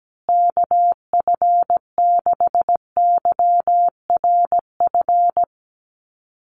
Morse code, heard as a keyed tone, about 17 wpm, 700 Hz. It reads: KF6YRF
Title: USA Call Signs Repeated 2x 17wpm